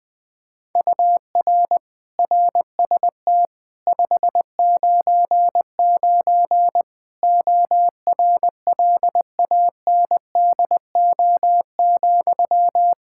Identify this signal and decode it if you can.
Morse code, heard as a keyed tone, about 20 wpm, 700 Hz. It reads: UR RST 599 ORLANDO,
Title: Practice QSO - Exchange 1 - Beginning - 20wpm